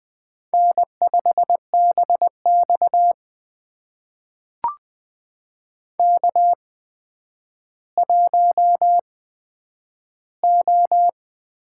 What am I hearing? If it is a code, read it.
Morse code, heard as a keyed tone, about 20 wpm, spaced as for 10 wpm, 700 Hz. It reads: N5BX K1O
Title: USA Call Signs - 8x Character Spacing - 20wpm